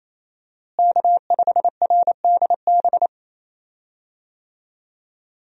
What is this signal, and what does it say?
Morse code, heard as a keyed tone, about 28 wpm, 700 Hz. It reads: K5RDB
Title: USA Call Signs - 5 Characters - Repeated 3x - 28wpm